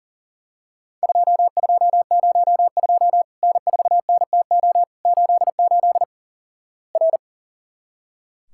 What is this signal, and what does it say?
Morse code, heard as a keyed tone, about 37 wpm, 700 Hz. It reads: 2202 N4NTO 88 R
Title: Worked All Europe DX Contest - QTC Exchange - 40wpm